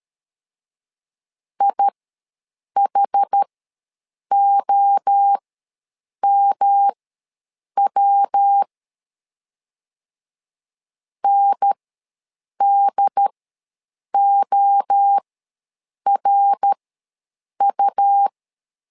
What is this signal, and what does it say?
Morse code, heard as a keyed tone, about 13 wpm, spaced as for 7 wpm, 800 Hz. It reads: IHOMW NDORU
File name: LEZIONI / 3 MWRO / 3 MWRO ws 12wpm 1914